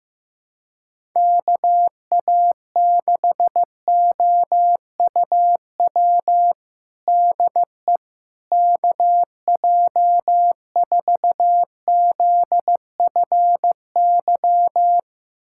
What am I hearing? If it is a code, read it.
Morse code, heard as a keyed tone, about 16 wpm, 700 Hz. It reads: KA6OUW DE KJ4ZFY
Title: USA Callsign DE Callsign - 6 Characters - 15wpm